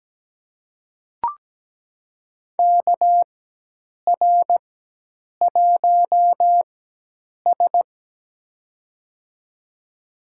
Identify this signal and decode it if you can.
Morse code, heard as a keyed tone, about 17 wpm, spaced as for 8 wpm, 700 Hz. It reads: KR1S
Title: USA Call Signs - 4x Character Spacing - 17wpm